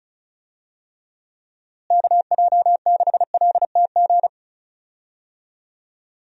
Morse code, heard as a keyed tone, about 36 wpm, 700 Hz. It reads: KJ6LTG
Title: USA Call Signs - 6 Characters - Repeated 2x - 35wpm